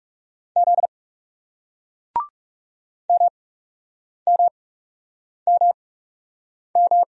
Morse code, heard as a keyed tone, about 39 wpm, spaced as for 8 wpm, 700 Hz. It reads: Z MMMM